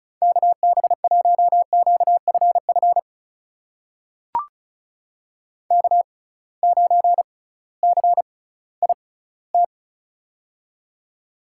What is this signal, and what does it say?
Morse code, heard as a keyed tone, about 34 wpm, spaced as for 19 wpm, 700 Hz. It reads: KB1QFF K9CIT
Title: USA Call Signs - 6x Character Spacing - 35wpm